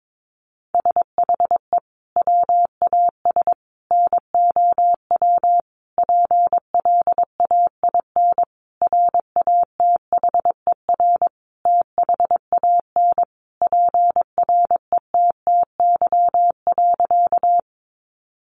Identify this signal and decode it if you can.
Morse code, heard as a keyed tone, about 22 wpm, 700 Hz. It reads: SHE WAS NOW PLAIN RATHER THAN PRETTY.
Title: Sentences from Top 1000 Words 22wpm